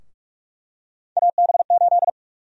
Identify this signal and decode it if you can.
Morse code, heard as a keyed tone, about 46 wpm, 700 Hz. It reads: AB8
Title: USA Call Sign Prefixes - 45wpm